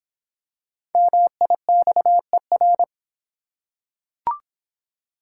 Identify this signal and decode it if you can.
Morse code, heard as a keyed tone, about 26 wpm, 700 Hz. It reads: MIXER